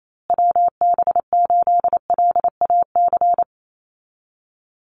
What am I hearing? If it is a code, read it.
Morse code, heard as a keyed tone, about 28 wpm, 700 Hz. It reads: WB8LAC
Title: USA Call Signs Repeated 2x 28wpm